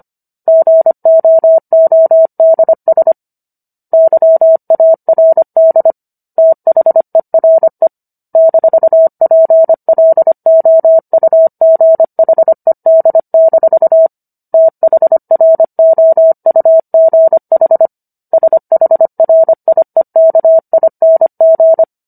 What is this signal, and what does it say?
Morse code, heard as a keyed tone, about 25 wpm, 650 Hz. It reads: GOODS YARD THERE -PLOUGHED- THROUGH SHRIEKING